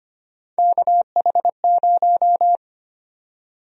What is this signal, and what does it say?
Morse code, heard as a keyed tone, about 25 wpm, 700 Hz. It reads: KH0
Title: USA Call Sign Prefixes - 25wpm